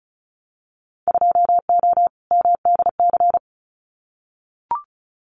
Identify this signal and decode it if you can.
Morse code, heard as a keyed tone, about 34 wpm, 700 Hz. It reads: 2O MDC
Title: Winter Field Day Exchanges 35wpm